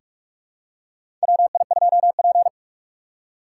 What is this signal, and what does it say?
Morse code, heard as a keyed tone, about 43 wpm, 700 Hz. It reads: WI2P
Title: USA Call Signs - 4x Character Spacing - 45wpm